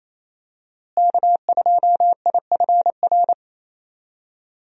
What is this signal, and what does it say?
Morse code, heard as a keyed tone, about 28 wpm, 700 Hz. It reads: K2IFR
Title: USA Call Signs - 5 Characters - Repeated 3x - 28wpm